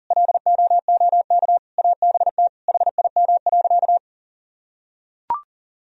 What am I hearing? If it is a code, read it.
Morse code, heard as a keyed tone, about 40 wpm, 700 Hz. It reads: LOOK ABT HIM.
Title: Sentences from Top 200 Words and Abbreviations 40wpm